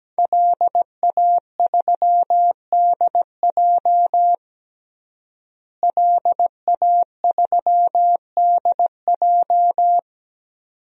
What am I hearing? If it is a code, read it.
Morse code, heard as a keyed tone, about 17 wpm, 700 Hz. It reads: LA3DJ LA3DJ
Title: DXCC-100 Call Signs - 5 Characters - Repeated 2x - 17wpm